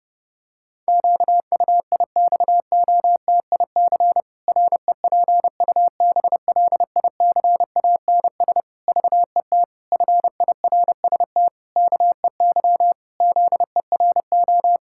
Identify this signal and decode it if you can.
Morse code, heard as a keyed tone, about 30 wpm, 700 Hz. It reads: QUIXOTIC REPUBLICANS VET FIRST KEY ZERO
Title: Pangram Challenge v2 - 30wpm